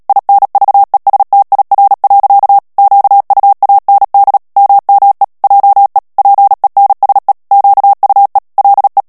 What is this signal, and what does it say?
Morse code, heard as a keyed tone, about 37 wpm, 800 Hz. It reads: INVESTIR. QUAND MME JE PENSE QUE LE
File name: CW / FICHIERS-MP3 / FRANCAIS TEXTE CLAIR 1 VITESSE 35 MOTS MINUTES